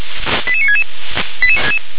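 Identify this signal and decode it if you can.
Five-tone selective call (ZVEI), heard as quick runs of five tones, tones of about 70 ms. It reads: 88966
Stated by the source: ZVEI five-tone selective call received on an SDR.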